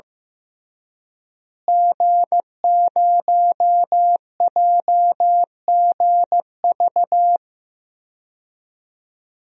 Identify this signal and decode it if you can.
Morse code, heard as a keyed tone, about 16 wpm, 700 Hz. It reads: G0JGV
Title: DXCC-100 Call Signs - 5 Characters - Repeated 3x - 15wpm